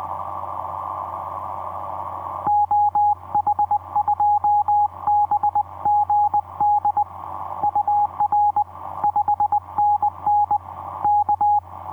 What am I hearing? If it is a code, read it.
Morse code, heard as a keyed tone, about 20 wpm, 840 Hz, with radio noise. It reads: OH2BGD UR 5NN K
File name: v5-dk1ce dm1tde etc 17m cw 28dec2010 0942